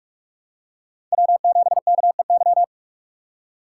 Morse code, heard as a keyed tone, about 46 wpm, 700 Hz. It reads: W7KEY
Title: USA Call Signs - 5 Characters - 4x Character Spacing - 45wpm